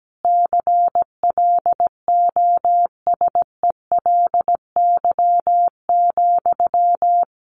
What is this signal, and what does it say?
Morse code, heard as a keyed tone, about 17 wpm, 700 Hz. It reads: CLOSELY,
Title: Sentences from Top 1400 Words 17wpm